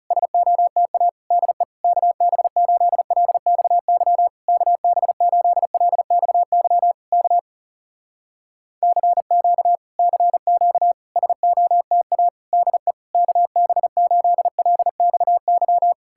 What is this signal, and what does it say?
Morse code, heard as a keyed tone, about 40 wpm, 700 Hz. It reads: SOTA DE KB8LXY KB8LXY K CQ CQ SOTA DE KB8LXY